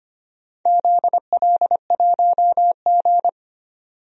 Morse code, heard as a keyed tone, about 25 wpm, 700 Hz. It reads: ZL1G
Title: DXCC-100 Call Signs - 4 Characters - 25wpm